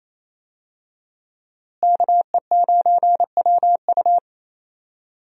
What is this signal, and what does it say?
Morse code, heard as a keyed tone, about 28 wpm, 700 Hz. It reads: KE9WU
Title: USA Call Signs - 5 Characters - Repeated 2x - 28wpm